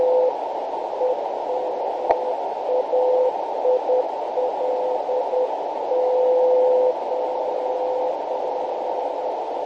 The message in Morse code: none